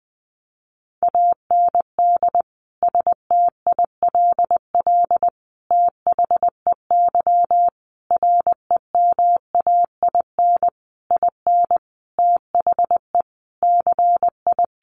Morse code, heard as a keyed tone, about 20 wpm, 700 Hz. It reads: AND STILL THEY REMAIN IN THE CI
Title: Sentences from Top 1200 Words 20wpm